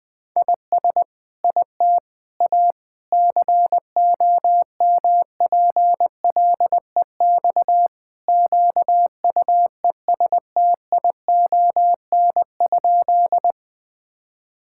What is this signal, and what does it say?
Morse code, heard as a keyed tone, about 20 wpm, 700 Hz. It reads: IS IT A COMPLEX QUESTION?
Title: Sentences from Top 1000 Words - Review Part 6 - 20wpm